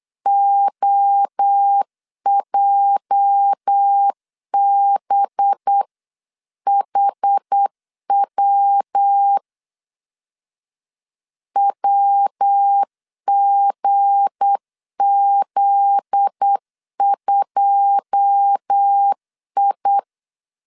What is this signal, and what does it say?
Morse code, heard as a keyed tone, about 8 wpm, 800 Hz. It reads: OJBHW WGZ2I
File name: CW / DET / 1 9 1500C S C 8wpm 5653